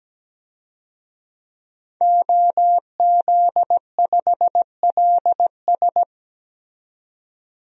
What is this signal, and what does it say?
Morse code, heard as a keyed tone, about 17 wpm, 700 Hz. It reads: OZ5LS
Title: DXCC-100 Call Signs - 5 Characters - 17wpm